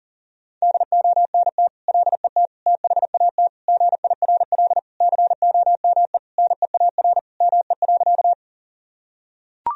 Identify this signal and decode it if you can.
Morse code, heard as a keyed tone, about 37 wpm, 700 Hz. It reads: DONT LET THAT GIRL COME NEAR ME.
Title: Sentences from Top 1000 Words - Review Part 1 - 40wpm